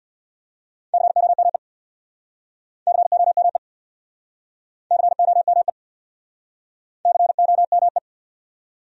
Morse code, heard as a keyed tone, about 69 wpm, 700 Hz. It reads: COME COME COME COME